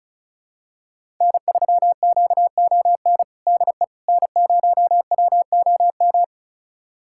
Morse code, heard as a keyed tone, about 34 wpm, 700 Hz. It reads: N3QON DE N0WOM
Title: USA Callsign DE Callsign - 5 Characters - 35wpm